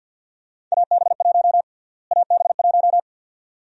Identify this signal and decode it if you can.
Morse code, heard as a keyed tone, about 51 wpm, 700 Hz. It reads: AB1 AB1
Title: USA Call Sign Prefixes - Repeated 2x - 50wpm